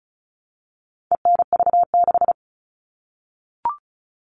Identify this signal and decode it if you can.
Morse code, heard as a keyed tone, about 34 wpm, 700 Hz. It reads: ENV6
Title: All Cut Numbers - 4-Digit 35wpm